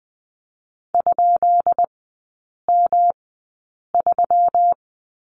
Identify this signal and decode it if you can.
Morse code, heard as a keyed tone, about 20 wpm, spaced as for 8 wpm, 700 Hz. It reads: ?M3
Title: 205 Lesson 33e - Letter j - Three Characters 20wpm